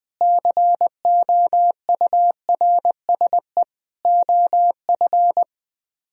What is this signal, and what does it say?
Morse code, heard as a keyed tone, about 20 wpm, 700 Hz. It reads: COURSE OF
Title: Speed-Racing - Sets of 4 Words 17wpm